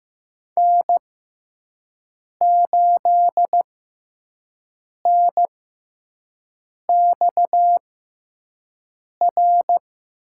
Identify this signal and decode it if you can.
Morse code, heard as a keyed tone, about 15 wpm, spaced as for 5 wpm, 700 Hz. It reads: N8NXR
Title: USA Call Signs - 6x Character Spacing - 15wpm